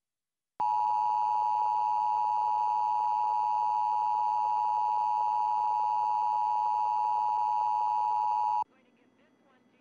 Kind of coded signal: Emergency Alert System (SAME) tones